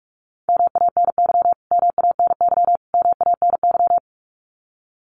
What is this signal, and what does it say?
Morse code, heard as a keyed tone, about 44 wpm, 700 Hz. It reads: MANY MANY MANY
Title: Top 500 Words Repeated 45wpm